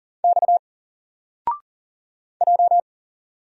Morse code, heard as a keyed tone, about 40 wpm, 700 Hz. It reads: X J
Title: Single Character (Letter:Number:Punctuation:Prosign) 40wpm